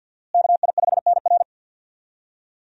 Morse code, heard as a keyed tone, about 50 wpm, 700 Hz. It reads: KI5NR
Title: USA Call Signs - 5 Characters - Repeated 2x - 50wpm